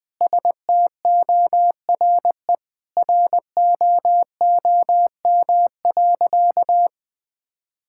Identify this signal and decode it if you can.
Morse code, heard as a keyed tone, about 21 wpm, 700 Hz. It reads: STORE ROOM.
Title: Sentences from Top 1000 Words - Review Part 5 - 20wpm